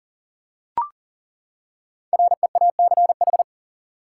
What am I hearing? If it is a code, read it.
Morse code, heard as a keyed tone, about 40 wpm, 700 Hz. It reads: REACH
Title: Five Letter Words - 40wpm